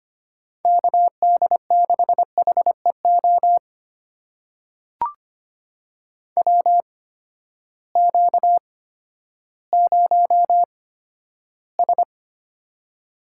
Morse code, heard as a keyed tone, about 25 wpm, spaced as for 13 wpm, 700 Hz. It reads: KD6HEO WQ0S
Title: USA Call Signs - 8x Character Spacing - 25wpm